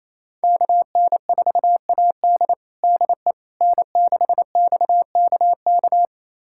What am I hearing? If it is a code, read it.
Morse code, heard as a keyed tone, about 28 wpm, 700 Hz. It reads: KN4AD DE N6XKK
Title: USA Callsign DE Callsign - 5 Characters - 28wpm